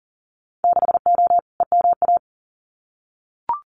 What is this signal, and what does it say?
Morse code, heard as a keyed tone, about 37 wpm, 700 Hz. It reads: 6O EMA
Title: Winter Field Day Exchanges Encore 40wpm